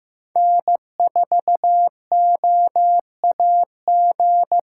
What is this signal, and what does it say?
Morse code, heard as a keyed tone, about 15 wpm, 700 Hz. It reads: N4OAG
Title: USA Callsign DE Callsign 15wpm